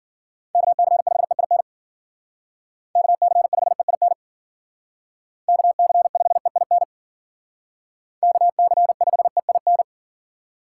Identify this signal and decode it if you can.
Morse code, heard as a keyed tone, about 50 wpm, 700 Hz. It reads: KC5EIN KC5EIN KC5EIN KC5EIN